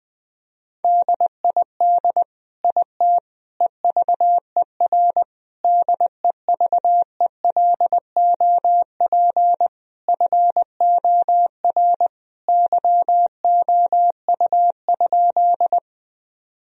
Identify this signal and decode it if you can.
Morse code, heard as a keyed tone, about 20 wpm, 700 Hz. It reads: DID IT EVER DEVELOP FOR YOU?
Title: Sentences from Top 1000 Words - Review Part 2 - 20wpm